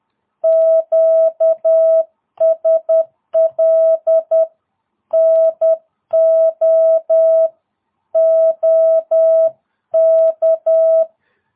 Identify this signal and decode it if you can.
Morse code, heard as a keyed tone, about 10 wpm, 650 Hz. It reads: QSL NO OK
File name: jh8cbh / 2022cw